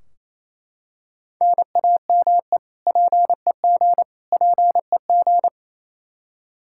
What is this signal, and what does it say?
Morse code, heard as a keyed tone, about 28 wpm, 700 Hz. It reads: NAME PEG PEG
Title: Practice QSO - Exchange 1 - Beginning - 28wpm